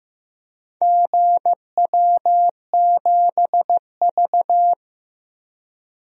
Morse code, heard as a keyed tone, about 15 wpm, 700 Hz. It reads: GW7V
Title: DXCC-100 Call Signs - 4 Characters - 15wpm